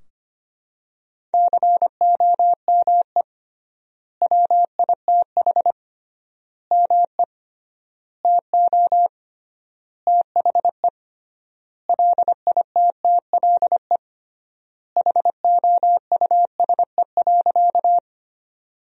Morse code, heard as a keyed tone, about 25 wpm, 700 Hz. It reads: COME WITH ME TO THE LITTLE HOUSE.
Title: 3x Word-Spacing - Sentences from Top 200 Words - 25wpm